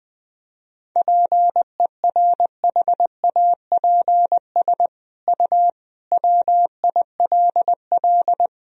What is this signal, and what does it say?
Morse code, heard as a keyed tone, about 20 wpm, 700 Hz. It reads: PERHAPS U WILL
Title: Sentences from Top 500 Words and Abbreviations 20wpm